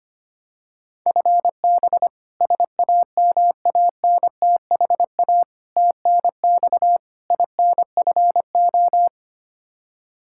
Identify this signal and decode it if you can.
Morse code, heard as a keyed tone, about 25 wpm, 700 Hz. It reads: FB SAMANTHA TNX INFO
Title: K1USN SST QSO Practice 25wpm